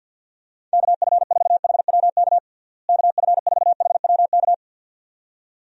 Morse code, heard as a keyed tone, about 50 wpm, 700 Hz. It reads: KF4HWK KF4HWK
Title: USA Call Signs - 6 Characters - Repeated 2x - 50wpm